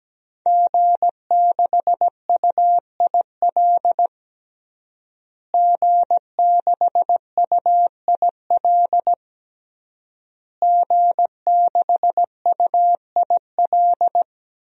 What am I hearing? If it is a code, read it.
Morse code, heard as a keyed tone, about 17 wpm, 700 Hz. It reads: G6UIL G6UIL G6UIL
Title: DXCC-100 Call Signs - 5 Characters - Repeated 3x - 17wpm